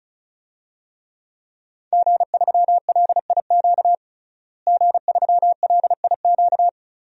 Morse code, heard as a keyed tone, about 34 wpm, 700 Hz. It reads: G3LIQ G3LIQ
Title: DXCC-100 Call Signs - 5 Characters - Repeated 3x - 35wpm